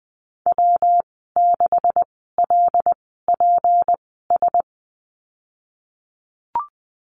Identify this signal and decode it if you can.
Morse code, heard as a keyed tone, about 19 wpm, spaced as for 14 wpm, 700 Hz. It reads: W6LPS
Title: 173 Lesson 28g - Number 9 - Call Signs 20wpm